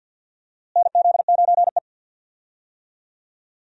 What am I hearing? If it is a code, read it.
Morse code, heard as a keyed tone, about 47 wpm, 700 Hz. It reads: NZ9E
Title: USA Call Signs - 4 Characters - Repeated 3x - 50wpm